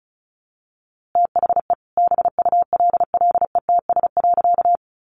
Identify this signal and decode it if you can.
Morse code, heard as a keyed tone, about 34 wpm, 700 Hz. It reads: THE BULLETS.
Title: A Sound Of Thunder 35wpm